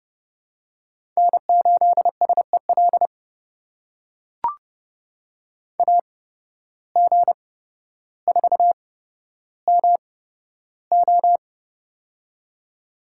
Morse code, heard as a keyed tone, about 30 wpm, spaced as for 13 wpm, 700 Hz. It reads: N8SEL AG4MO
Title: USA Call Signs - 8x Character Spacing - 30wpm